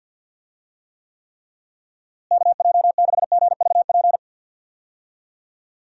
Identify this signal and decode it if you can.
Morse code, heard as a keyed tone, about 50 wpm, 700 Hz. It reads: KJ6GVP
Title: USA Call Signs - 6 Characters - Repeated 3x - 50wpm